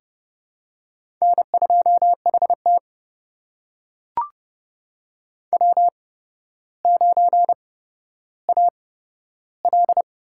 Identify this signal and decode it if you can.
Morse code, heard as a keyed tone, about 31 wpm, spaced as for 13 wpm, 700 Hz. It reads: N2HT W9AL
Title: USA Call Signs - 4 Characters - 8x Character Spacing - 30wpm